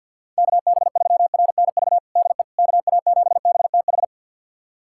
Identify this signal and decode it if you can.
Morse code, heard as a keyed tone, about 52 wpm, 700 Hz. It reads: KB3RNV DE KA7BTH
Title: USA Callsign DE Callsign 50wpm